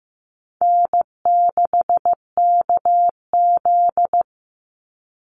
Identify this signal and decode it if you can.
Morse code, heard as a keyed tone, about 15 wpm, 700 Hz. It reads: N6KZ
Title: NA SOTA Chaser Callsigns 15wpm